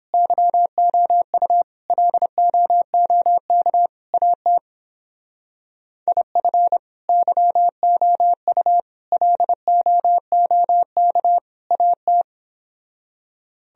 Morse code, heard as a keyed tone, about 29 wpm, 700 Hz. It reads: YOU LOOK AT IF YOU LOOK AT